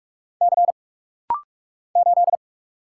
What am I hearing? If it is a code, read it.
Morse code, heard as a keyed tone, about 48 wpm, 700 Hz. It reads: C 8
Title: Single Letter-Number - Rapid-Fire 45wpm